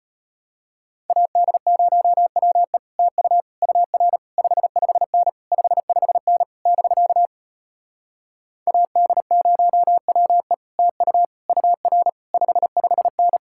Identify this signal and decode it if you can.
Morse code, heard as a keyed tone, about 33 wpm, 700 Hz. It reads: AD0WE TU UR 55N 55N <BK> AD0WE TU UR 55N